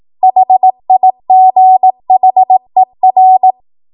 Morse code, heard as a keyed tone, about 18 wpm, 750 Hz. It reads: HIGHER
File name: Morse / 18WPM